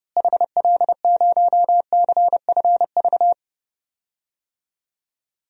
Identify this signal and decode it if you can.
Morse code, heard as a keyed tone, about 29 wpm, 700 Hz. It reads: HL0CFV
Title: DXCC-100 Call Signs - 6 Characters - Repeated 2x - 30wpm